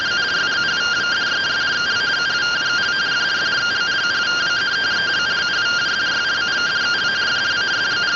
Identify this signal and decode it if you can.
Two-tone teleprinter signal, about 170 Hz shift.